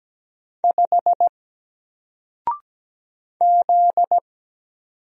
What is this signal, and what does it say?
Morse code, heard as a keyed tone, about 17 wpm, 700 Hz. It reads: 5 Z
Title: Single Letter-Number Alt 17wpm